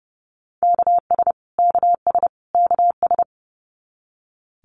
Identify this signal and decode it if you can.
Morse code, heard as a keyed tone, about 30 wpm, 700 Hz. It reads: KS KS KS
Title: US States Repeated 3x 30wpm